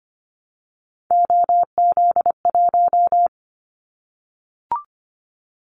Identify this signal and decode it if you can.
Morse code, heard as a keyed tone, about 25 wpm, 700 Hz. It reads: OZ1
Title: Callsign Prefixes for Top 50 DXCC - 25wpm